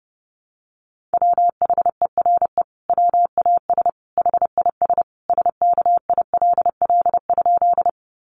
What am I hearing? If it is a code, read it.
Morse code, heard as a keyed tone, about 30 wpm, 700 Hz. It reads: WHERE WAS HIS SKILL?
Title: Sentences from Top 1200 Words 30wpm